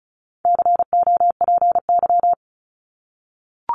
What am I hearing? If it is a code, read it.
Morse code, heard as a keyed tone, about 34 wpm, 700 Hz. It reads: COPY
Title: Top 1251-1500 Words 35wpm